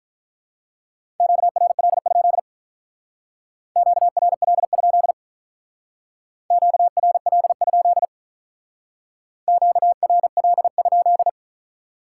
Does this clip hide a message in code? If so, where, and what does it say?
Yes: Morse code from 1.2 to 11.3 s, heard as a keyed tone, about 45 wpm, 700 Hz. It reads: QRL? QRL? QRL? QRL?